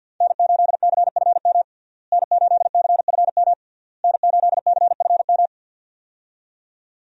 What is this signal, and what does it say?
Morse code, heard as a keyed tone, about 46 wpm, 700 Hz. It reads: N8CFM N8CFM N8CFM
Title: USA Call Signs - 5 Characters - Repeated 3x - 50wpm